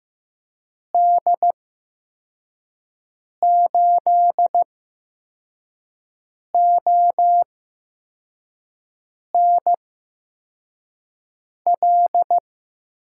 Morse code, heard as a keyed tone, about 15 wpm, spaced as for 4 wpm, 700 Hz. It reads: D8ONL
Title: USA Call Signs - 6 Characters - 8x Character Spacing - 15wpm